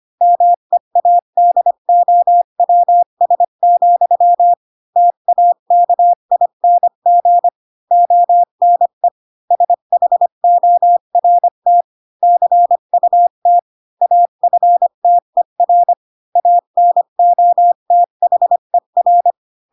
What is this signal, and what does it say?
Morse code, heard as a keyed tone, about 25 wpm, 700 Hz. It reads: MEADOWS, TAKING ONE SHORT CUT AFTER ANOTHER
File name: Pinocchio / Pinocchio ch4 20 25 ik1hge